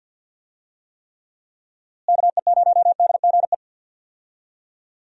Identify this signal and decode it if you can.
Morse code, heard as a keyed tone, about 47 wpm, 700 Hz. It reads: KE0DGE
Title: USA Call Signs - 6 Characters - Repeated 2x - 50wpm